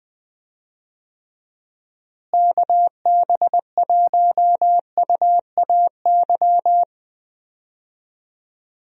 Morse code, heard as a keyed tone, about 20 wpm, 700 Hz. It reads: KB1UAY
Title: USA Call Signs - 6 Characters - Repeated 2x - 20wpm